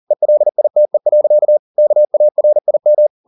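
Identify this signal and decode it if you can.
Morse code, heard as a keyed tone, about 40 wpm, 600 Hz. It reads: ELITE. KARIM
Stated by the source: Morse code at 40 wpm, practice text, CW med Gnister.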